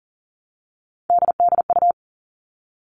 Morse code, heard as a keyed tone, about 38 wpm, 700 Hz. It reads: DDU
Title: All Cut Numbers - 3-Digit 40wpm